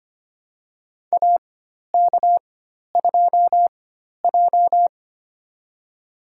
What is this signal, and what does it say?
Morse code, heard as a keyed tone, about 25 wpm, spaced as for 12 wpm, 690 Hz. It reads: AK2J
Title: USA Call Signs - 4 Characters - 4x Character Spacing - 25wpm